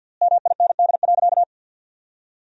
Morse code, heard as a keyed tone, about 50 wpm, 700 Hz. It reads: MIND.
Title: 4x Word-Spacing - Sentences from Top 500 Words - 50wpm